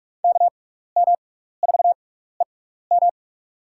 Morse code, heard as a keyed tone, about 42 wpm, spaced as for 15 wpm, 700 Hz. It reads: KM4EM
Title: USA Call Signs - 6x Character Spacing - 45wpm